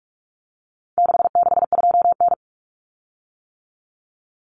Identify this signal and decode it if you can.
Morse code, heard as a keyed tone, about 44 wpm, 700 Hz. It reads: 662N
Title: Common Cut Numbers - 4-Digit 45wpm